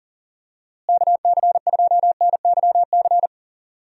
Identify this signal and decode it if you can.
Morse code, heard as a keyed tone, about 40 wpm, 700 Hz. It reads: KC2NYC
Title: USA Call Signs - 6 Characters - Repeated 3x - 40wpm